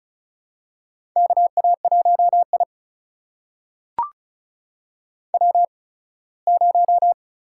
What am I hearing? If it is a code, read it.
Morse code, heard as a keyed tone, about 36 wpm, spaced as for 21 wpm, 700 Hz. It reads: KA1I W0
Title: USA Call Signs - 8x Character Spacing - 35wpm